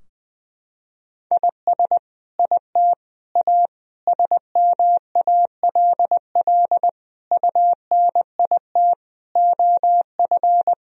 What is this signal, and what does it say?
Morse code, heard as a keyed tone, about 20 wpm, 700 Hz. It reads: IS IT A SMALL UNIT OF
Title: Sentences from Top 1000 Words - Review Part 2 - 20wpm